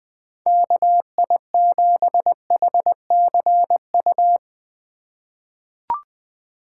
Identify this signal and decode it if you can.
Morse code, heard as a keyed tone, about 20 wpm, 700 Hz. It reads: KI7HCU
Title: USA Call Signs - 6 Characters - 2x Character Spacing - 20wpm